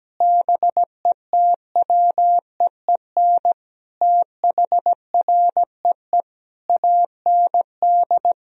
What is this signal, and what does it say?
Morse code, heard as a keyed tone, about 17 wpm, 700 Hz. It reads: BETWEEN THREE AND
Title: Sentences from Top 1000 Words - Review Part 3 - 17wpm